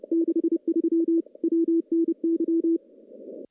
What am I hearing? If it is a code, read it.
Morse code, heard as a keyed tone, about 30 wpm, 330 Hz. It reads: 63 WNY
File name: contests / CW-SS-k5tr- -1200